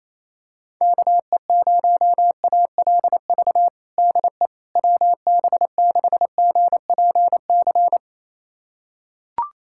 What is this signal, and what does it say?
Morse code, heard as a keyed tone, about 28 wpm, 700 Hz. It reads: KE0ALV DE WB6GPC
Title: USA Callsign DE Callsign - 6 Characters - 28wpm